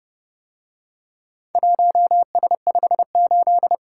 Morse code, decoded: 1S58

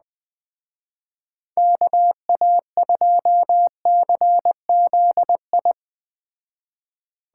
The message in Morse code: KA2CZI

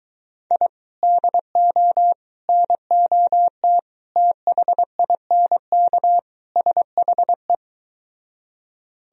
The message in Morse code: I DO NOT THINK SHE

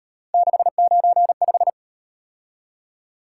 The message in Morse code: 695

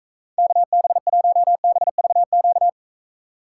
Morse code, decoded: KB1BVQ

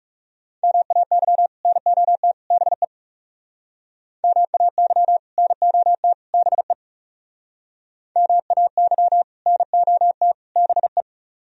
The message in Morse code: MAY NOT BE MAY NOT BE MAY NOT BE

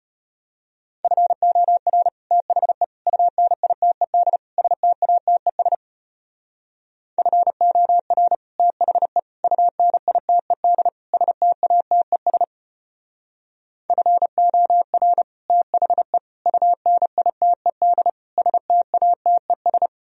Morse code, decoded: FOR THE UNITED STATES FOR THE UNITED STATES FOR THE UNITED STATES

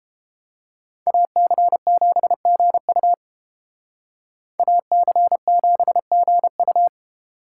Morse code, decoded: AC7GU AC7GU

